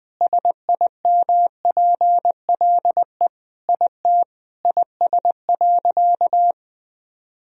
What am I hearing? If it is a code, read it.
Morse code, heard as a keyed tone, about 20 wpm, 700 Hz. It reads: SIMPLE IT IS.
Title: Sentences from Top 400 Words and Abbreviations 20wpm